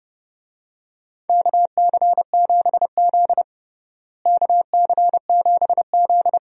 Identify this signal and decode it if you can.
Morse code, heard as a keyed tone, about 30 wpm, 700 Hz. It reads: KC7Z KC7Z
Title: USA Call Signs - 4 Characters - Repeated 2x - 30wpm